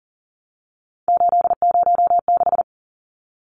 Morse code, heard as a keyed tone, about 40 wpm, 700 Hz. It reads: 806
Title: Numbers - 3-Digit 40wpm